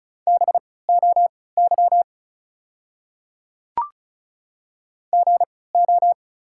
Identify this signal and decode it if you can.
Morse code, heard as a keyed tone, about 34 wpm, spaced as for 20 wpm, 690 Hz. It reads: BOY GO
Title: 3x Character Spacing - Top 200 Words - 35wpm